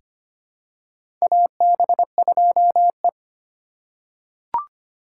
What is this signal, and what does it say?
Morse code, heard as a keyed tone, about 25 wpm, 700 Hz. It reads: AB2E
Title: USA Call Signs - 4 Characters - 2x Character Spacing - 25wpm